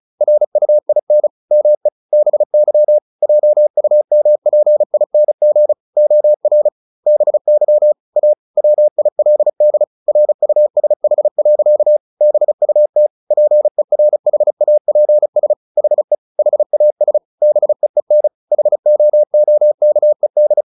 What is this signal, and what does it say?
Morse code, heard as a keyed tone, about 34 wpm, 600 Hz. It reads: RUIN ME BY JUMPING OR BY A WILD RUSH. BUT PERHAPS HE HAS BEEN HOOKED
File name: the old man and the sea / the old man and the sea 1 35wpm